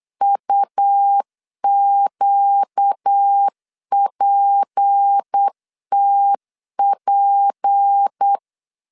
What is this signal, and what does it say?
Morse code, heard as a keyed tone, about 8 wpm, 800 Hz. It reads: UQPTP
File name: LEZIONI / 6 PKX0 / 6 P 8wpm 3025